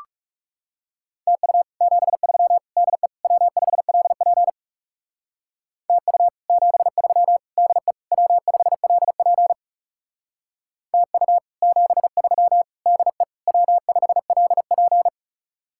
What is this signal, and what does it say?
Morse code, heard as a keyed tone, about 39 wpm, 700 Hz. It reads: TU 73 DE W5LP TU 73 DE W5LP TU 73 DE W5LP